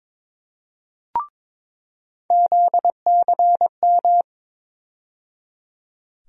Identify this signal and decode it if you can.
Morse code, heard as a keyed tone, about 22 wpm, 700 Hz. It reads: ZCM
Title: USA Call Sign Postfixes - 3 Letters 22wpm